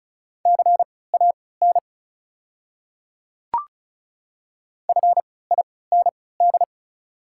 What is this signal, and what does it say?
Morse code, heard as a keyed tone, about 36 wpm, spaced as for 20 wpm, 690 Hz. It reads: CAN FIND
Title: 3x Character Spacing - Top 200 Words - 35wpm